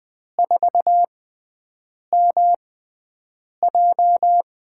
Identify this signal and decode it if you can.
Morse code, heard as a keyed tone, about 20 wpm, spaced as for 7 wpm, 700 Hz. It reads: 4MJ
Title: USA Call Signs - 6x Character Spacing - 20wpm